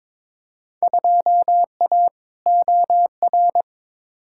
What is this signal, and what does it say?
Morse code, heard as a keyed tone, about 22 wpm, 700 Hz. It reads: 2A OR
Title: Field Day Exchanges - 22wpm